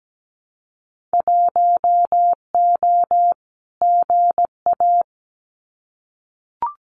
Morse code, heard as a keyed tone, about 17 wpm, 700 Hz. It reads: 1O GA
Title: Winter Field Day Exchanges 17wpm